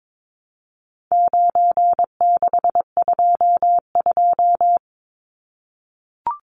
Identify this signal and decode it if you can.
Morse code, heard as a keyed tone, about 22 wpm, 700 Hz. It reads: 9622
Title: Numbers - 4-Digit 22wpm